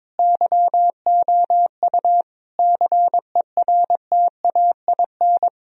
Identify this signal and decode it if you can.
Morse code, heard as a keyed tone, about 22 wpm, 700 Hz. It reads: YOU CERTAIN
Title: Sentences from Top 1000 Words - Review Part 3 - 22wpm